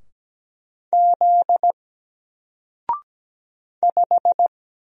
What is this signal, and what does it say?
Morse code, heard as a keyed tone, about 17 wpm, 700 Hz. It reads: Z 5
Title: Single Character (Letter:Number:Punctuation:Prosign) Alt 17wpm